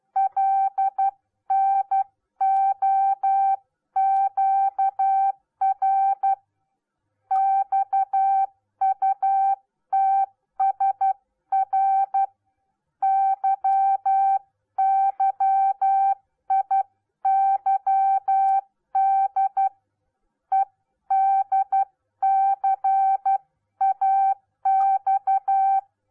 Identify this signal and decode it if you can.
Morse code, heard as a keyed tone, about 12 wpm, 780 Hz. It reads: LNOQR XUTSR YYIYD EDCAX